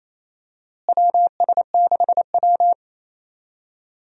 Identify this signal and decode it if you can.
Morse code, heard as a keyed tone, about 28 wpm, 700 Hz. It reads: WS6W
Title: USA Call Signs - 4 Characters - Repeated 3x - 28wpm